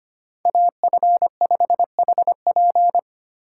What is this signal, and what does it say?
Morse code, heard as a keyed tone, about 30 wpm, 700 Hz. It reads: AF5HP
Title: Speed Racing - USA Call Signs 25wpm